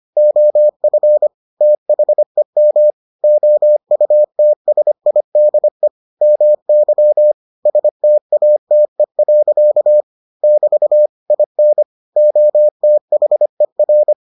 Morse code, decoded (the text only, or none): OF THEM OUTSIDE MY STATE. <BT> IN OTHER